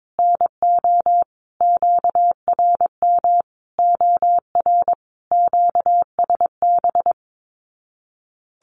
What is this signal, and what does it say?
Morse code, heard as a keyed tone, about 22 wpm, 700 Hz. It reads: NO QRM OR QSB
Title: Common QSO Phrases 22wpm